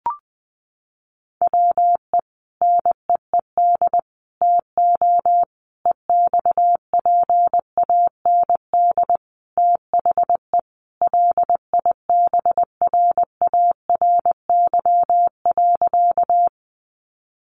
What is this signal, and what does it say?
Morse code, heard as a keyed tone, about 20 wpm, 700 Hz. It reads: WE NEED TO EXPAND THE LIBRARY.
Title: Sentences from Top 1400 Words 20wpm